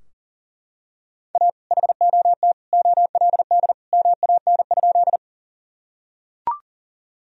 Morse code, heard as a keyed tone, about 40 wpm, 700 Hz. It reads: A HOT OLD MAN?
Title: Sentences from Top 1000 Words - Review Part 3 - 40wpm